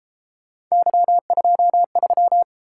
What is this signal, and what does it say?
Morse code, heard as a keyed tone, about 33 wpm, 700 Hz. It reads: Y23